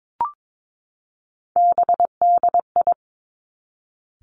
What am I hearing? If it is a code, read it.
Morse code, heard as a keyed tone, about 22 wpm, 700 Hz. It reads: BDI
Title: Three Challenging Letter-Number Combinations 22wpm